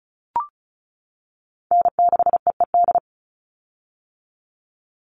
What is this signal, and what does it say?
Morse code, heard as a keyed tone, about 34 wpm, 700 Hz. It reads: N6EED
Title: Call Signs - 35wpm